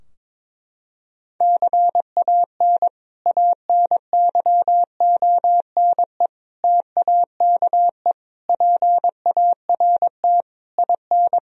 Morse code, heard as a keyed tone, about 22 wpm, 700 Hz. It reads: CAN ANYONE TAKE PART IN
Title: Sentences from Top 1000 Words - Review Part 4 - 22wpm